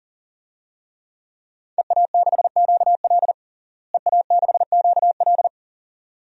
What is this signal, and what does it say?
Morse code, heard as a keyed tone, about 40 wpm, 700 Hz. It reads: EA6QL EA6QL
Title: DXCC-100 Call Signs - 5 Characters - Repeated 2x - 40wpm